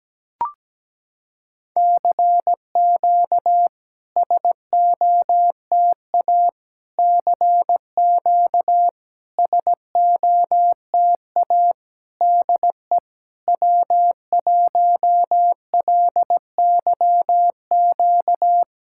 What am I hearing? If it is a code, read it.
Morse code, heard as a keyed tone, about 17 wpm, 700 Hz. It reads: CQ SOTA CQ SOTA DE W1LYQ
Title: SOTA QSO Practice 17wpm